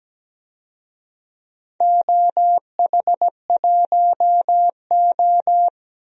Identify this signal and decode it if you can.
Morse code, heard as a keyed tone, about 17 wpm, 700 Hz. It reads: OH1O
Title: DXCC-100 Call Signs - 4 Characters - 17wpm